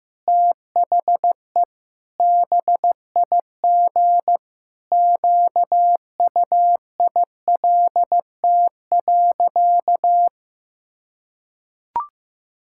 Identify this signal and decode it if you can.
Morse code, heard as a keyed tone, about 15 wpm, 700 Hz. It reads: THE BIG QUILT.
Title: Pangram Challenge v2 - 15wpm